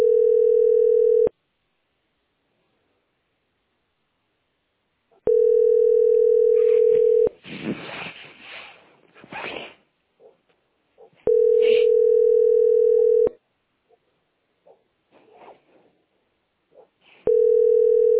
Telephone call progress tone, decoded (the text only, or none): ringback tone